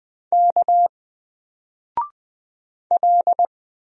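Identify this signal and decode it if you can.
Morse code, heard as a keyed tone, about 20 wpm, 700 Hz. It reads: K L
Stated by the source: Single letter and number Morse drill at 20 wpm, from N4LCD.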